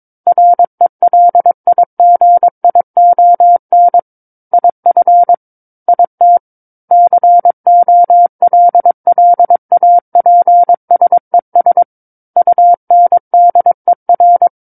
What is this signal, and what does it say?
Morse code, heard as a keyed tone, about 22 wpm, 700 Hz. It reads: RELIGION IF IT COLLAPSES UNDER
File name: Book One / 13-Chapter 13 - 22 WPM